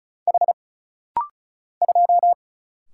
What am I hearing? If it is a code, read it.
Morse code, heard as a keyed tone, about 33 wpm, 700 Hz. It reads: H 2
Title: Single Letter-Number - Rapid-Fire 35wpm